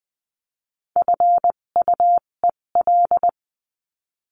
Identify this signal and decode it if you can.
Morse code, heard as a keyed tone, about 20 wpm, spaced as for 17 wpm, 700 Hz. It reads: FUEL
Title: 257 Lesson 42b - Words 14-20wpm